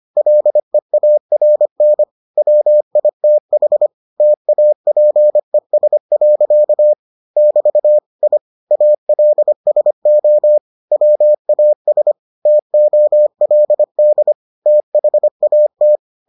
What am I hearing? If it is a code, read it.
Morse code, heard as a keyed tone, about 25 wpm, 600 Hz. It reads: LEARN WITH TAPES. <BT> I ALSO WAS TOLD THAT